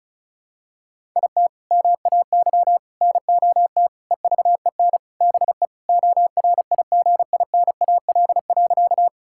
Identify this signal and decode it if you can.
Morse code, heard as a keyed tone, about 36 wpm, 700 Hz. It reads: IT MAY NOT EVEN BE ORIGINAL.
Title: Sentences from Top 1000 Words - Review Part 6 - 35wpm